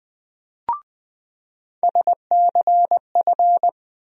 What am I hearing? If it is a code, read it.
Morse code, heard as a keyed tone, about 20 wpm, 700 Hz. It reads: SCF
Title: USA Call Sign Postfixes - 3 Letters 20wpm